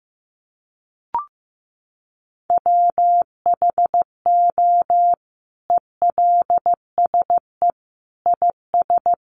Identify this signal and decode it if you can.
Morse code, heard as a keyed tone, about 15 wpm, 700 Hz. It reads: WHO ELSE IS
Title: Sentences from Top 800 Words 15wpm